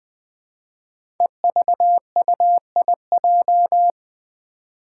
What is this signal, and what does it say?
Morse code, heard as a keyed tone, about 21 wpm, 700 Hz. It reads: EVUIJ
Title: Five Challenging Letter-Number Combinations 20wpm